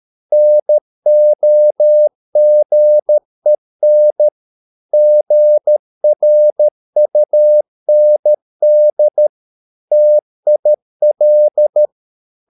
Morse code, heard as a keyed tone, about 13 wpm, 600 Hz. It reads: NOGEN GRUND TIL
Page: Ferie 13 wpm